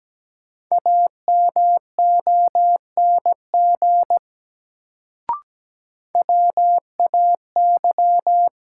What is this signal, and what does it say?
Morse code, heard as a keyed tone, about 17 wpm, 700 Hz. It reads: AMONG WAY
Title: Sets of 2 Words - New Word Drill - 17wpm